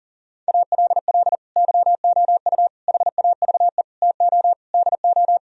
Morse code, heard as a keyed tone, about 52 wpm, 700 Hz. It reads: ALL YOU HAVE TO DO